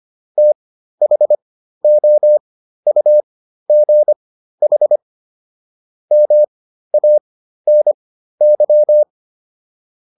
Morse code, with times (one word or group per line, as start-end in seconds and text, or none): 0.4-5.0 THOUGH
6.1-9.0 MANY